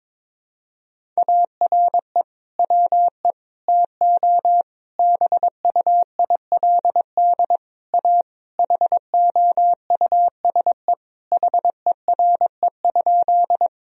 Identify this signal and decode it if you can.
Morse code, heard as a keyed tone, about 22 wpm, 700 Hz. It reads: ARE WE TO BUILD A HOUSE HERE?
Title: Sentences from Top 1000 Words - Review Part 1 - 22wpm